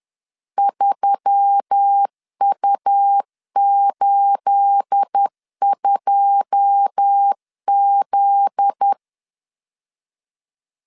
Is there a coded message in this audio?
Yes: Morse code from 0.6 to 8.9 s, heard as a keyed tone, about 10 wpm, 800 Hz. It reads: 3U82Z